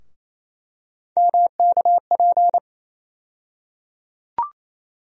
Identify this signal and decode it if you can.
Morse code, heard as a keyed tone, about 28 wpm, 700 Hz. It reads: MKP
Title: USA Call Sign Postfixes - 3 Letters 28wpm